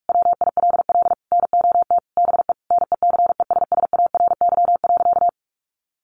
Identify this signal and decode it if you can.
Morse code, heard as a keyed tone, about 44 wpm, 700 Hz. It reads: WILL NOT BE NECESSARY.
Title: Sentences from Top 800 Words 45wpm